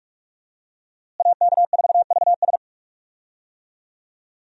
Morse code, heard as a keyed tone, about 46 wpm, 700 Hz. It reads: AK4VS
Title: USA Call Signs - 5 Characters - Repeated 3x - 45wpm